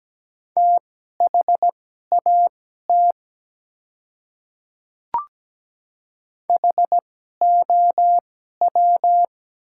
Morse code, heard as a keyed tone, about 17 wpm, spaced as for 12 wpm, 700 Hz. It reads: THAT HOW
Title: 2x Character Spacing - Top 100 Words - 17wpm